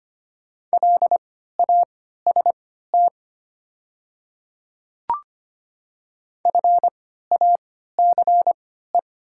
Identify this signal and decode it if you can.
Morse code, heard as a keyed tone, about 25 wpm, spaced as for 14 wpm, 700 Hz. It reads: LAST FACE
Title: 3x Character Spacing - Top 300 Words - 25wpm